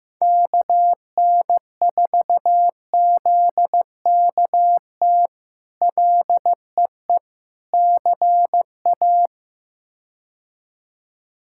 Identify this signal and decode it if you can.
Morse code, heard as a keyed tone, about 15 wpm, 700 Hz. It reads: KN4ZKT LEE CA
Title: K1USN SST QSO Practice - 15wpm